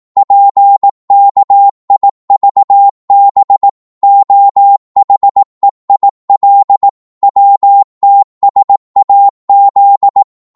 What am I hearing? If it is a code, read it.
Morse code, heard as a keyed tone, about 18 wpm, 800 Hz. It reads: PKIVB OHEIL WTSAZ